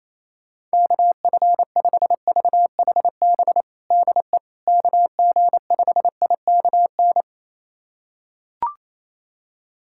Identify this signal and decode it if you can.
Morse code, heard as a keyed tone, about 28 wpm, 700 Hz. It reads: KF5VHB DE KG5IKN
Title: USA Callsign DE Callsign - 6 Characters - 28wpm